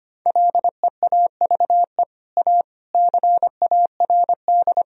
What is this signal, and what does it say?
Morse code, heard as a keyed tone, about 25 wpm, 700 Hz. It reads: LEAVE A CARD
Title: Sentences from Top 1000 Words - Review Part 4 - 25wpm